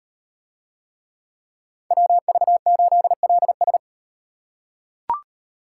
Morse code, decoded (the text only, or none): WV8LS